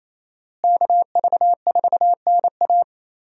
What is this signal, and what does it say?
Morse code, heard as a keyed tone, about 28 wpm, 700 Hz. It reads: KV4NA
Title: USA Call Signs - 5 Characters - Repeated 2x - 28wpm